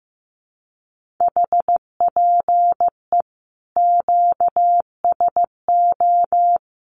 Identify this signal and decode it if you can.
Morse code, heard as a keyed tone, about 15 wpm, 700 Hz. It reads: HPE QSO
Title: Common QSO Phrases 15wpm